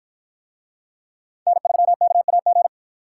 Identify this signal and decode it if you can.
Morse code, heard as a keyed tone, about 51 wpm, 700 Hz. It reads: N3KAG